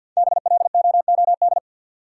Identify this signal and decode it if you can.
Morse code, heard as a keyed tone, about 53 wpm, 700 Hz. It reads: BLOOD
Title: Sentences from Top 1000 Words - Review Part 2 - New Word Drill - 50wpm